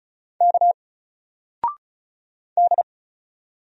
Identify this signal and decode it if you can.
Morse code, heard as a keyed tone, about 36 wpm, 700 Hz. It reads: K D
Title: Single Letter-Number 35wpm